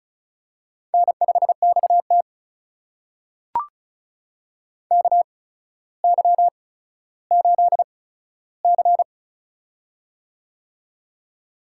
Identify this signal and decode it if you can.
Morse code, heard as a keyed tone, about 36 wpm, spaced as for 15 wpm, 700 Hz. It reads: N5XT KY8C
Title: USA Call Signs - 4 Characters - 8x Character Spacing - 35wpm